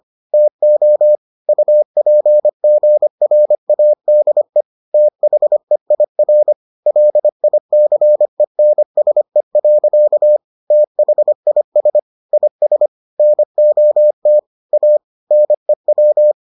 Morse code, heard as a keyed tone, about 25 wpm, 600 Hz. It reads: TO UPGRADE THEIR LICENSE. THIS IS NOT A NEW